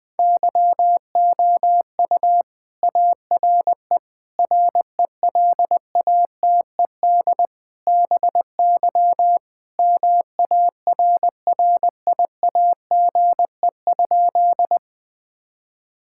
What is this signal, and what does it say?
Morse code, heard as a keyed tone, about 21 wpm, 700 Hz. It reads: YOU ARE RELATED BY MARRIAGE?
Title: Sentences from Top 1000 Words - Review Part 5 - 20wpm